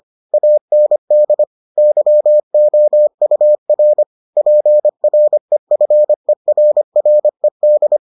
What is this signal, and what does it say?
Morse code, heard as a keyed tone, about 25 wpm, 600 Hz. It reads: AND YOUR PREFERRED